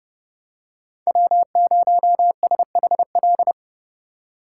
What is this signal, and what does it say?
Morse code, heard as a keyed tone, about 30 wpm, 700 Hz. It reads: W0SHL
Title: K1USN SST QSO Practice - 30wpm